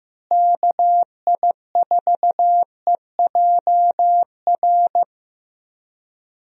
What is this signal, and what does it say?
Morse code, heard as a keyed tone, about 15 wpm, 700 Hz. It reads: KI4EJR
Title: USA Call Signs - 6 Characters - Repeated 3x - 15wpm